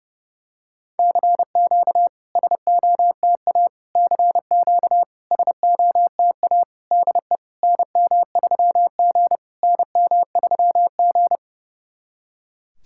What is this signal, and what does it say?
Morse code, heard as a keyed tone, about 30 wpm, 690 Hz. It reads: CQ SOTA CQ SOTA DE NM3G NM3G
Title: SOTA QSO Practice 30wpm